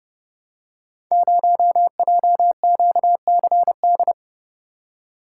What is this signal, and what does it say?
Morse code, heard as a keyed tone, about 30 wpm, 700 Hz. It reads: 0JQCD
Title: Code Groups - 5 Letter:Numbers - 30wpm